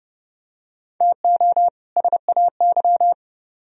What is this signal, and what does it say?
Morse code, heard as a keyed tone, about 30 wpm, 700 Hz. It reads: TO SAY